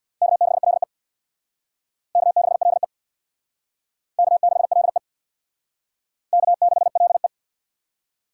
Morse code, heard as a keyed tone, about 60 wpm, 700 Hz. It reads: K6LE K6LE K6LE K6LE
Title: Speed Racing - USA 1x2 Call Signs 50wpm